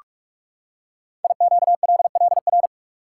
Morse code, heard as a keyed tone, about 44 wpm, 700 Hz. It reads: IQLLR
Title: Code Groups - 5 Letters - 45wpm